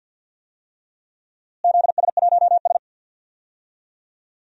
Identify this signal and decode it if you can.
Morse code, heard as a keyed tone, about 52 wpm, 700 Hz. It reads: ZS1S
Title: DXCC-100 Call Signs - 4 Characters - Repeated 2x - 50wpm